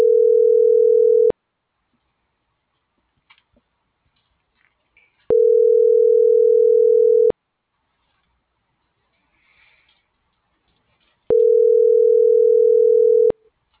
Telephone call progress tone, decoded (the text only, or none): ringback tone